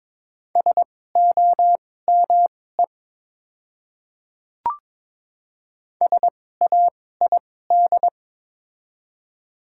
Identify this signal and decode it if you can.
Morse code, heard as a keyed tone, about 22 wpm, spaced as for 16 wpm, 700 Hz. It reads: SOME SAID
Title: 2x Character Spacing - Top 100 Words - 22wpm